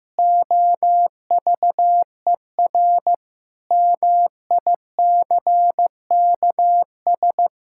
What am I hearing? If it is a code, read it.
Morse code, heard as a keyed tone, about 16 wpm, 700 Hz. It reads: OVER MICKS
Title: Pangram Challenge v2 - 15wpm